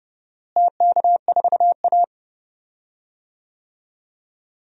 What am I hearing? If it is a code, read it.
Morse code, heard as a keyed tone, about 31 wpm, 700 Hz. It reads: TK4A
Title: DXCC-100 Call Signs - 4 Characters - Repeated 2x - 30wpm